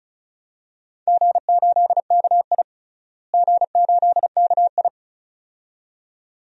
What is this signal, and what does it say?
Morse code, heard as a keyed tone, about 36 wpm, 700 Hz. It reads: G8KI G8KI
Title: DXCC-100 Call Signs - 4 Characters - Repeated 2x - 35wpm